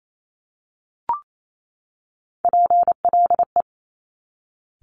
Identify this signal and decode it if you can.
Morse code, heard as a keyed tone, about 28 wpm, 700 Hz. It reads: PLE
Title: Top Three Letter Combinations in Words 28wpm